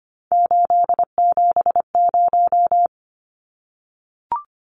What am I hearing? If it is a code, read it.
Morse code, heard as a keyed tone, about 25 wpm, 700 Hz. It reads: 870
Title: Numbers - 3-Digit 25wpm